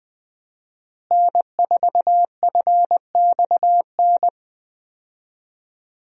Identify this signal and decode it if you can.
Morse code, heard as a keyed tone, about 21 wpm, 700 Hz. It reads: N4FXN
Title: USA Call Signs - 5 Characters - Repeated 2x - 20wpm